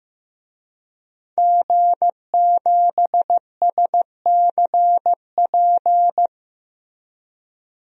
Morse code, heard as a keyed tone, about 16 wpm, 700 Hz. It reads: G7SCP
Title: DXCC-100 Call Signs - 5 Characters - Repeated 2x - 15wpm